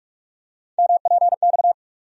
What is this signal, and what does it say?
Morse code, heard as a keyed tone, about 46 wpm, 700 Hz. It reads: MPX
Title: USA Call Sign Postfixes - 3 Letters 45wpm